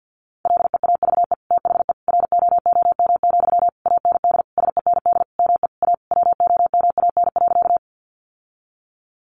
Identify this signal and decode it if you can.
Morse code, heard as a keyed tone, about 50 wpm, 700 Hz. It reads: LEAVE THE ROOM, AND SEND ME A WOMAN.
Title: Sentences from Top 1500 Words 50wpm